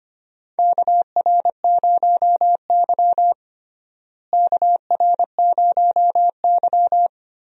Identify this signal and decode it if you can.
Morse code, heard as a keyed tone, about 25 wpm, 700 Hz. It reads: KR0Y KR0Y
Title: USA Call Signs - 4 Characters - Repeated 2x - 25wpm